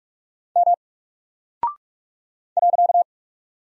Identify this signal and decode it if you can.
Morse code, heard as a keyed tone, about 44 wpm, 700 Hz. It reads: M .
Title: Single Character (Letter:Number:Punctuation:Prosign) 45wpm